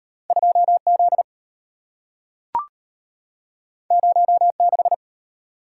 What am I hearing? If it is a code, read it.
Morse code, heard as a keyed tone, about 34 wpm, 700 Hz. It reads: 2Z 06